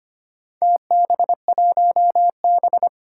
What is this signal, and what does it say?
Morse code, heard as a keyed tone, about 25 wpm, 700 Hz. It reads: TB1B
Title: Four Challenging Letter-Number Combinations 25wpm